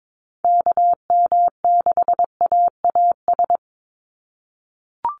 Morse code, heard as a keyed tone, about 22 wpm, 700 Hz. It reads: KM6AAS
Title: USA Call Signs 22wpm